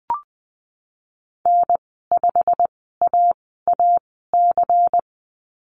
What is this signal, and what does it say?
Morse code, heard as a keyed tone, about 20 wpm, spaced as for 15 wpm, 700 Hz. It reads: N5AAC
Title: 122 Lesson 20g - Number 3 - Call Signs 20wpm